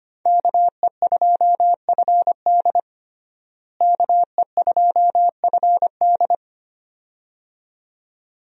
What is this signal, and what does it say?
Morse code, heard as a keyed tone, about 25 wpm, 700 Hz. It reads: KE2FD KE2FD
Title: USA Call Signs - 5 Characters - Repeated 2x - 25wpm